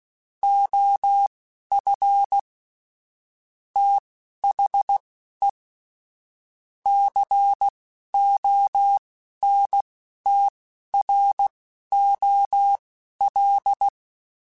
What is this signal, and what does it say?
Morse code, heard as a keyed tone, about 16 wpm, spaced as for 11 wpm, 800 Hz. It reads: OF THE CONTROL
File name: Technical Subjects / Technical Subjects-mp3s / OR Gate @10